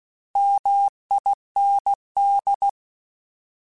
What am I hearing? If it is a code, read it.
Morse code, heard as a keyed tone, about 16 wpm, 800 Hz. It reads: MIND
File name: Alphabetical / Alphabetical mp3s / Ms @15